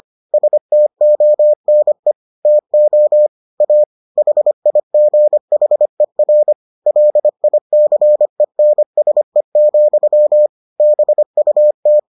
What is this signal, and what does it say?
Morse code, heard as a keyed tone, about 25 wpm, 600 Hz. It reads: STONE TO A HIGHER LICENSE, BUT